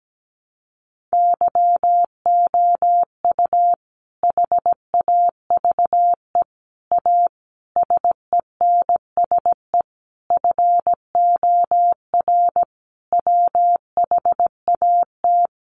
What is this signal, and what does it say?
Morse code, heard as a keyed tone, about 17 wpm, 700 Hz. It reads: YOU HAVE A SENSE FOR WHAT
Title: Sentences from Top 700 Words 17wpm